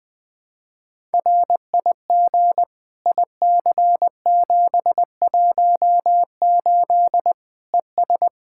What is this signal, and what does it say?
Morse code, heard as a keyed tone, about 20 wpm, 700 Hz. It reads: RIG IC718 ES
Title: Practice QSO - Exchange 2 - Info - 20wpm